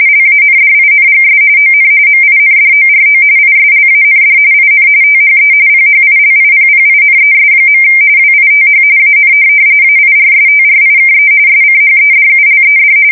RTTY, about 45.45 baud. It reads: LANDINGS ON MINDORO ISLAND IN THE PHILIPPINES. WITH AN AIRSTRIKE SCHEDULED
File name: fsk170 / USS-New-Jersey